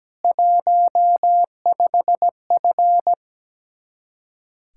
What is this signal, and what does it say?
Morse code, heard as a keyed tone, about 17 wpm, 700 Hz. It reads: 15F
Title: Practice QSO - Exchange 2 - Info - 17wpm